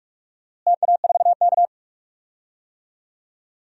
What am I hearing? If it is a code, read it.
Morse code, heard as a keyed tone, about 46 wpm, 700 Hz. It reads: TA4K
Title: DXCC-100 Call Signs - 4 Characters - Repeated 2x - 45wpm